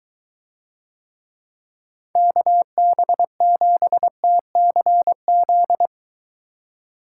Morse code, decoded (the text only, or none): KB7TCZ